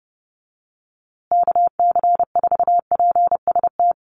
Morse code, heard as a keyed tone, about 29 wpm, 700 Hz. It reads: KC4PST
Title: USA Call Signs Repeated 3x 30wpm